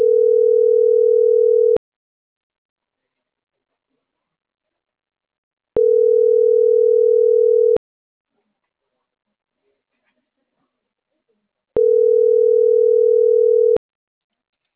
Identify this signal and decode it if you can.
Ringback tone (North America), 440 + 480 Hz.